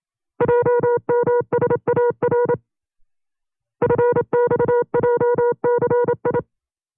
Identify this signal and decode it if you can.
Morse code, heard as a keyed tone, about 27 wpm, 480 Hz. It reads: JMSAR FXJCI
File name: CW-practice/25WPM